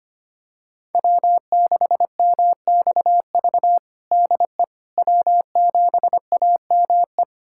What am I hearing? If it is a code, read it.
Morse code, heard as a keyed tone, about 25 wpm, 700 Hz. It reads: W6MXV DE W7AME
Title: USA Callsign DE Callsign - 5 Characters - 25wpm